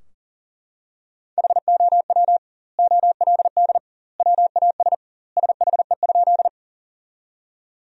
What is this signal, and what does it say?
Morse code, heard as a keyed tone, about 40 wpm, 700 Hz. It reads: HOW OLD WAS SHE?
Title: 2x Word-Spacing - Sentences from Top 200 Words - 40wpm